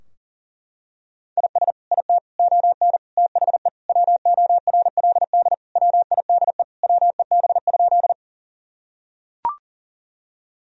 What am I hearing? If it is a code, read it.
Morse code, heard as a keyed tone, about 37 wpm, 700 Hz. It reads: IS IT ON THE WORLD WIDE WEB?
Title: Sentences from Top 1000 Words - Review Part 4 - 40wpm